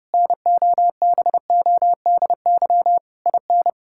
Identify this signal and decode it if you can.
Morse code, heard as a keyed tone, about 32 wpm, 700 Hz. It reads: NOBODY IN
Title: Sentences from Top 1600 Words 30wpm